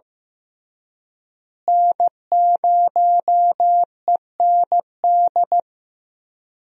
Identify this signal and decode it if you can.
Morse code, heard as a keyed tone, about 15 wpm, 700 Hz. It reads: N0END
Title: USA Call Signs - 5 Characters - Repeated 3x - 15wpm